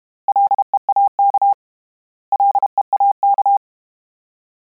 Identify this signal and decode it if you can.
Morse code, heard as a keyed tone, about 32 wpm, 800 Hz. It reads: LEAK LEAK
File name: Combinations / Combinations-mp3s / Combinations@30